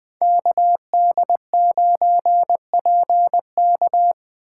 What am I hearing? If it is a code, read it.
Morse code, heard as a keyed tone, about 21 wpm, 700 Hz. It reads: KD9PK
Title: USA Call Signs - 5 Characters - 2x Character Spacing - 20wpm